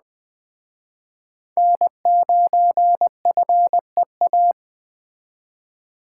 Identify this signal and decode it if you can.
Morse code, heard as a keyed tone, about 20 wpm, 700 Hz. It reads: N9FEA
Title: Speed Racing - USA 1x3 Call Signs 17wpm